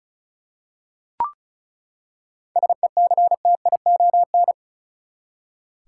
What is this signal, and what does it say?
Morse code, heard as a keyed tone, about 36 wpm, 700 Hz. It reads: SECTION
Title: Seven Letter Words - 35wpm